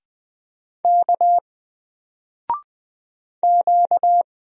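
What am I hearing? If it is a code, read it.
Morse code, heard as a keyed tone, about 20 wpm, 700 Hz. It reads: K Q
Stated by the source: Single letter and number Morse drill at 20 wpm, from N4LCD.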